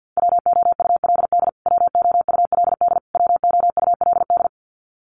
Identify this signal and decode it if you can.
Morse code, heard as a keyed tone, about 50 wpm, 700 Hz. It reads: WOULD WOULD WOULD
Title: Top 300 Words Repeated 50wpm